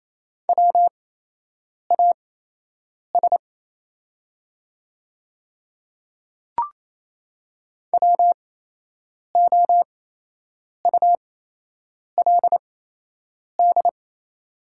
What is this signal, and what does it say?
Morse code, heard as a keyed tone, about 28 wpm, spaced as for 8 wpm, 700 Hz. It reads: WAS WOULD
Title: 8x Character Spacing - Top 100 Words - 28wpm